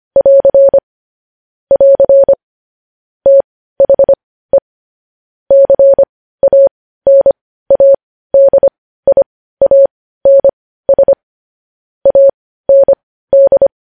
Morse code, heard as a keyed tone, about 25 wpm, spaced as for 15 wpm, 550 Hz. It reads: <AR> <AR> THE CANADIANS AND
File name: podcast / News headlines-1x-25x15